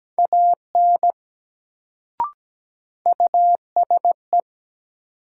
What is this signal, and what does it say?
Morse code, heard as a keyed tone, about 17 wpm, 700 Hz. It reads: AN USE
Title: Sets of 2 Words - New Word Drill - 17wpm